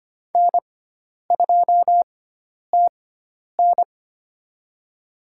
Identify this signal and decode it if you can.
Morse code, heard as a keyed tone, about 25 wpm, spaced as for 10 wpm, 700 Hz. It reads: N2TN
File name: Lesson 16g - Letter c - Call Signs 25wpm